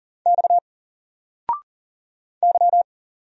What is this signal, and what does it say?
Morse code, heard as a keyed tone, about 40 wpm, 700 Hz. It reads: X Y
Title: Single Letter-Number Alt 40wpm